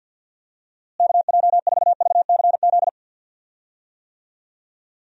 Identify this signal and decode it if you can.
Morse code, heard as a keyed tone, about 52 wpm, 700 Hz. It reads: KJ4VCZ
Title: USA Call Signs - 6 Characters - Repeated 2x - 50wpm